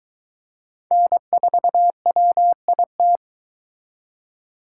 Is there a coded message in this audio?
Yes: Morse code from 0.9 to 3.2 s, heard as a keyed tone, about 22 wpm, 700 Hz. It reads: N4WIT